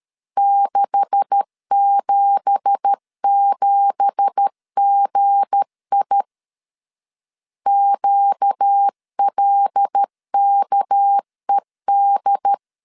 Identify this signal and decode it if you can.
Morse code, heard as a keyed tone, about 13 wpm, 800 Hz. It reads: 677GI QLKED
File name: LEZIONI / 9 2867 12wpm 2322